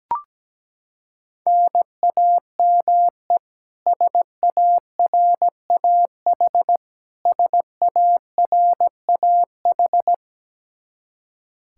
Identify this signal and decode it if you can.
Morse code, heard as a keyed tone, about 17 wpm, 700 Hz. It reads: NAME SARAH SARAH
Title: Practice QSO - Exchange 1 - Beginning - 17wpm